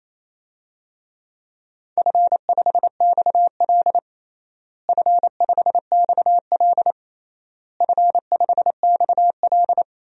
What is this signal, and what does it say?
Morse code, heard as a keyed tone, about 28 wpm, 700 Hz. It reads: F5XL F5XL F5XL
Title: DXCC-100 Call Signs - 4 Characters - Repeated 3x - 28wpm